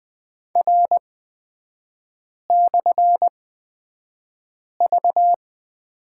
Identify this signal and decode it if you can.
Morse code, heard as a keyed tone, about 20 wpm, spaced as for 5 wpm, 700 Hz. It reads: R/V
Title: 255 Lesson 41e - Period - Three Characters 20wpm